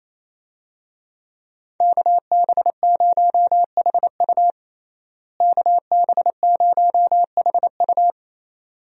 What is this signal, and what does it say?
Morse code, heard as a keyed tone, about 28 wpm, 700 Hz. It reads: KB0HU KB0HU
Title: USA Call Signs - 5 Characters - Repeated 2x - 28wpm